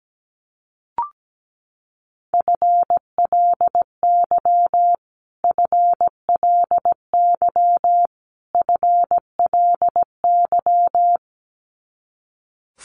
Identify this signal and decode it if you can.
Morse code, heard as a keyed tone, about 17 wpm, 700 Hz. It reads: FLY FLY FLY
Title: Top 500 Words Repeated 17wpm